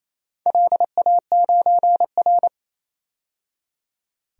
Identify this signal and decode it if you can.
Morse code, heard as a keyed tone, about 28 wpm, 700 Hz. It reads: LA9R
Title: DXCC-100 Call Signs - 4 Characters - 28wpm